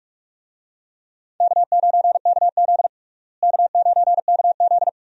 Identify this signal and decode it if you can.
Morse code, heard as a keyed tone, about 46 wpm, 700 Hz. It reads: K9KZ K9KZ
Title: USA Call Signs - 4 Characters - Repeated 3x - 45wpm